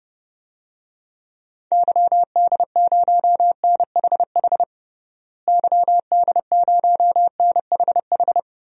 Morse code, heard as a keyed tone, about 30 wpm, 700 Hz. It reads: YD0NHH YD0NHH
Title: DXCC-100 Call Signs - 6 Characters - Repeated 2x - 30wpm